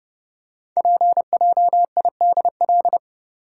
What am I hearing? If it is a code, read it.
Morse code, heard as a keyed tone, about 30 wpm, 700 Hz. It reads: PJIDL
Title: Code Groups - 5 Letters - 30wpm